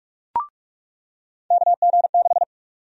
Morse code, heard as a keyed tone, about 43 wpm, 700 Hz. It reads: KG6
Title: USA Call Sign Prefixes - 45wpm